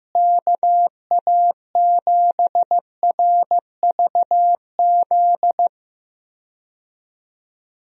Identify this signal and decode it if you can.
Morse code, heard as a keyed tone, about 15 wpm, 700 Hz. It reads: KA7RVZ
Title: USA Call Signs - 6 Characters - Repeated 2x - 15wpm